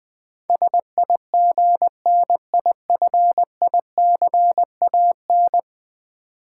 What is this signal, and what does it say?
Morse code, heard as a keyed tone, about 21 wpm, 700 Hz. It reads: SIGNIFICAN
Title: Build A Word - 5 Letters At A Time - 12-Letter Words - 20wpm